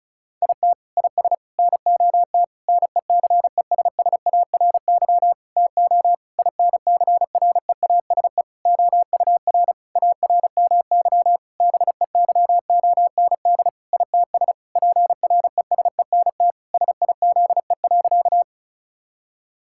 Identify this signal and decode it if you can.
Morse code, heard as a keyed tone, about 34 wpm, 700 Hz. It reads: IT IS NOT NECESSARY TO INCREASE OUR ARMY BEYOND ITS PRESENT SIZE.
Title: Sentences from Top 1000 Words - Review Part 6 - 35wpm